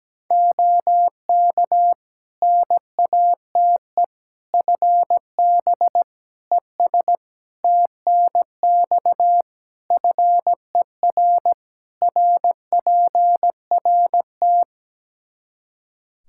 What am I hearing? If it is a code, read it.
Morse code, heard as a keyed tone, about 17 wpm, 700 Hz. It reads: OK NATE FB ES TNX FER RPRT
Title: Practice QSO - Exchange 2 - Info - 17wpm